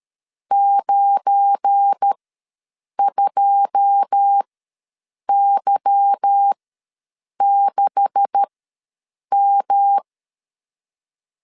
Morse code, heard as a keyed tone, about 13 wpm, spaced as for 7 wpm, 800 Hz. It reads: 92Y6M
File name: LEZIONI / 9 286 ws 12wpm 3351